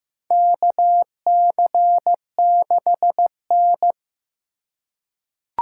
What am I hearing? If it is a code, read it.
Morse code, heard as a keyed tone, about 15 wpm, 700 Hz. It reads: KC6N
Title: USA Call Signs - 2x Character Spacing - 15wpm